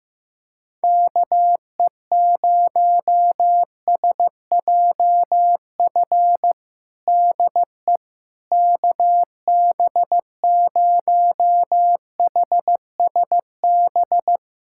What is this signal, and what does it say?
Morse code, heard as a keyed tone, about 16 wpm, 700 Hz. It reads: KE0SJF DE KB0HSB
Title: USA Callsign DE Callsign - 6 Characters - 15wpm